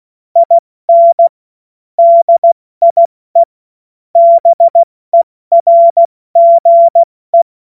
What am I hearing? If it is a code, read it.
Morse code, heard as a keyed tone, about 16 wpm, spaced as for 14 wpm, 680 Hz. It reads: IN DIE BERGE
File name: Morsekurs / Witze / Witz 24 Juni Mai 2025 70 F80